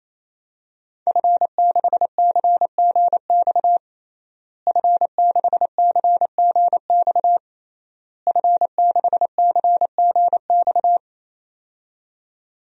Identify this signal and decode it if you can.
Morse code, heard as a keyed tone, about 28 wpm, 700 Hz. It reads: F6CGX F6CGX F6CGX
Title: DXCC-100 Call Signs - 5 Characters - Repeated 3x - 28wpm